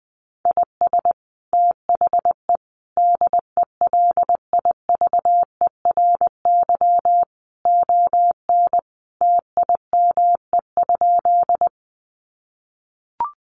Morse code, heard as a keyed tone, about 20 wpm, 700 Hz. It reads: IS THE DELIVERY ON TIME?
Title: Sentences from Top 1500 Words 20wpm